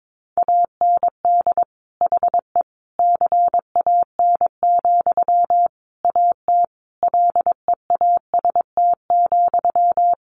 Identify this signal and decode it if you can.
Morse code, heard as a keyed tone, about 22 wpm, 700 Hz. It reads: AND HE CAN, AT LEAST,
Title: Sentences from Top 900 Words 22wpm